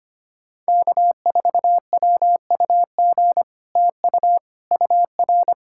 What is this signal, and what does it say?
Morse code, heard as a keyed tone, about 25 wpm, 700 Hz. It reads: K4WUG TU UR
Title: Speed-Racing Common QSO Phrases - 25wpm